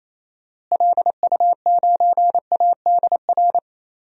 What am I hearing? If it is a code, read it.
Morse code, heard as a keyed tone, about 28 wpm, 700 Hz. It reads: LU9ADR
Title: DXCC-100 Call Signs - 6 Characters - Repeated 2x - 28wpm